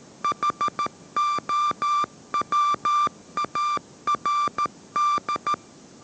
Morse code, heard as a keyed tone, about 13 wpm, 1180 Hz. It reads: HOWARD
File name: 14wpm